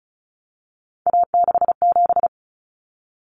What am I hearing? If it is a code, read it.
Morse code, heard as a keyed tone, about 34 wpm, 700 Hz. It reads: A67
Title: Common Cut Numbers - 3-Digit 35wpm